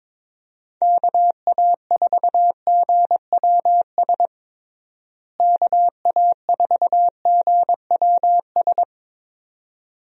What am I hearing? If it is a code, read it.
Morse code, heard as a keyed tone, about 22 wpm, 700 Hz. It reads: KA4GWS KA4GWS
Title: USA Call Signs - 6 Characters - Repeated 2x - 22wpm